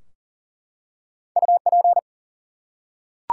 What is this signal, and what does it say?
Morse code, heard as a keyed tone, about 40 wpm, 700 Hz. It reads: UP
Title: Two Letter Words - 40wpm